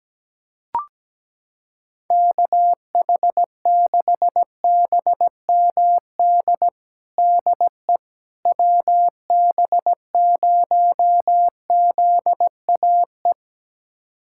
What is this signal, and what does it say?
Morse code, heard as a keyed tone, about 17 wpm, 700 Hz. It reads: KH6BMD DE WB0ZAE
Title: USA Callsign DE Callsign - 6 Characters - 17wpm